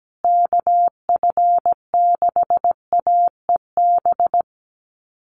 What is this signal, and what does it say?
Morse code, heard as a keyed tone, about 17 wpm, 700 Hz. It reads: KF6AEB
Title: USA Call Signs Repeated 3x 17wpm